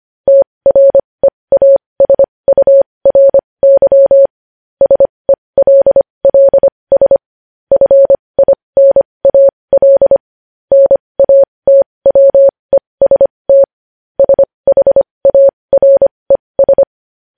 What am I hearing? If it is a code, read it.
Morse code, heard as a keyed tone, about 25 wpm, spaced as for 20 wpm, 550 Hz. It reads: TREASURY SELLS FINAL NATWEST SHARES
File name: podcast / News headlines-1x-25x20